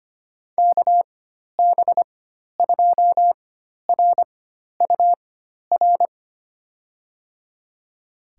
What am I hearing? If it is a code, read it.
Morse code, heard as a keyed tone, about 25 wpm, spaced as for 12 wpm, 700 Hz. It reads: KB2RUR
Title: USA Call Signs - 6 Characters - 4x Character Spacing - 25wpm